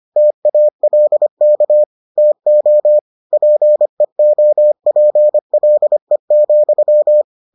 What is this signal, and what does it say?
Morse code, heard as a keyed tone, about 25 wpm, 600 Hz. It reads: TALK TO PEOPLE,